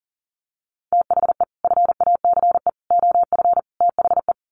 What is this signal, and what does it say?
Morse code, heard as a keyed tone, about 40 wpm, 700 Hz. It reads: THE FACE OF THE
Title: Sets of 4 Words 40wpm